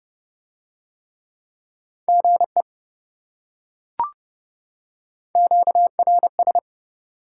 Morse code, heard as a keyed tone, about 28 wpm, 700 Hz. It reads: GE QRS